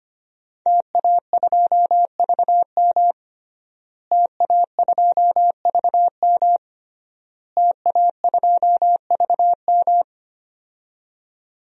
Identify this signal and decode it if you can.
Morse code, heard as a keyed tone, about 25 wpm, 700 Hz. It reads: TA2VM TA2VM TA2VM
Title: DXCC-100 Call Signs - 5 Characters - Repeated 3x - 25wpm